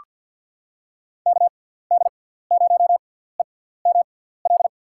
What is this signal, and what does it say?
Morse code, heard as a keyed tone, about 46 wpm, spaced as for 17 wpm, 700 Hz. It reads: KD0EML
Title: USA Call Signs - 6 Characters - 6x Character Spacing - 50wpm